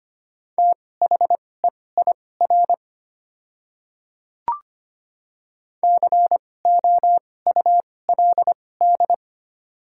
Morse code, heard as a keyed tone, about 25 wpm, spaced as for 18 wpm, 700 Hz. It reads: THEIR COULD
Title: 2x Character Spacing - Top 500 Words - 25wpm